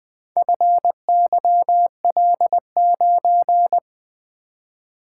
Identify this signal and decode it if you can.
Morse code, heard as a keyed tone, about 20 wpm, 700 Hz. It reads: FYL9
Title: Four Letter-Number Combinations - 20wpm